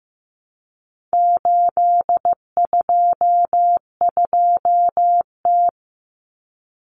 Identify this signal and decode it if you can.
Morse code, heard as a keyed tone, about 15 wpm, 700 Hz. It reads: 822T
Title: Common Cut Numbers - 4-Digit 15wpm